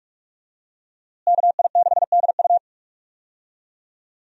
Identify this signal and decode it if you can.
Morse code, heard as a keyed tone, about 44 wpm, 700 Hz. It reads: KI6DU
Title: USA Call Signs - 5 Characters - Repeated 2x - 45wpm